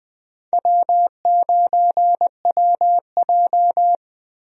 Morse code, heard as a keyed tone, about 20 wpm, 700 Hz. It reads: W9WJ
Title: USA Call Signs - 4 Characters - Repeated 3x - 20wpm